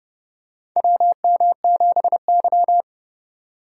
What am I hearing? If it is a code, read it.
Morse code, heard as a keyed tone, about 30 wpm, 700 Hz. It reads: WM7Y
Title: USA Call Signs - 4 Characters - 3x Character Spacing - 30wpm